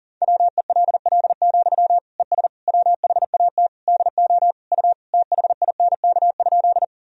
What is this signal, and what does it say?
Morse code, heard as a keyed tone, about 40 wpm, 700 Hz. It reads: WELL, ES WHAT DO U THINK?
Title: Sentences from Top 200 Words and Abbreviations 40wpm